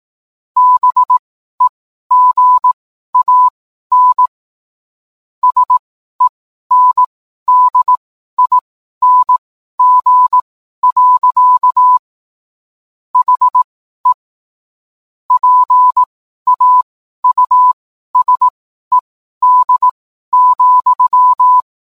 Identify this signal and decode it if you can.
Morse code, heard as a keyed tone, about 18 wpm, spaced as for 12 wpm, 1000 Hz. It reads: BEGAN SENDING. HE PAUSED,